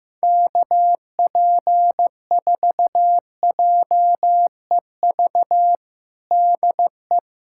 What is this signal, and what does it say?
Morse code, heard as a keyed tone, about 15 wpm, 700 Hz. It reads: KP4JEV DE
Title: USA Callsign DE Callsign 15wpm